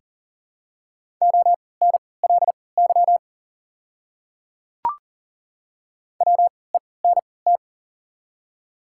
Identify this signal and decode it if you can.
Morse code, heard as a keyed tone, about 40 wpm, spaced as for 23 wpm, 690 Hz. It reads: ONLY WENT
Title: 3x Character Spacing - Top 300 Words - 40wpm